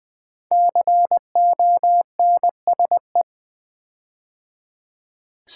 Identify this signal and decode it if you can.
Morse code, heard as a keyed tone, about 20 wpm, 700 Hz. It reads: CONSE
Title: Build A Word - 5 Letters At A Time - 12-Letter Words - 20wpm